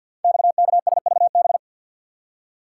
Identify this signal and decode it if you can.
Morse code, heard as a keyed tone, about 51 wpm, 690 Hz. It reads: XKSVB
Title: Code Groups - 5 Letters - 50wpm